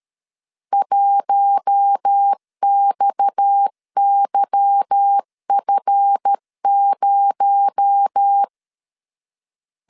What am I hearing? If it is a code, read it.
Morse code, heard as a keyed tone, about 13 wpm, 800 Hz. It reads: 1XYF0
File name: LEZIONI / 8 YJ45 / 8 YJ 12wpm 2235